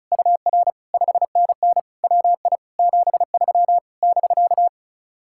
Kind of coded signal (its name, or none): Morse code